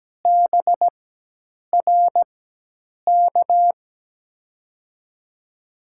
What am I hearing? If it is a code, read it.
Morse code, heard as a keyed tone, about 17 wpm, spaced as for 8 wpm, 700 Hz. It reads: BRK
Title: USA Call Signs - 4x Character Spacing - 17wpm